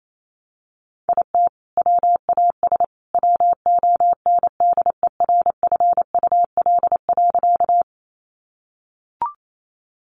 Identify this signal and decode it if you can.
Morse code, heard as a keyed tone, about 28 wpm, 700 Hz. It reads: IT WAS WONDERFUL.
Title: Sentences from Top 1100 Words 28wpm